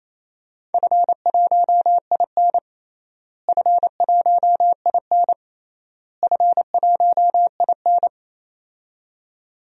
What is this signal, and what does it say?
Morse code, heard as a keyed tone, about 27 wpm, 700 Hz. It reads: F1IN F1IN F1IN
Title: DXCC-100 Call Signs - 4 Characters - Repeated 3x - 28wpm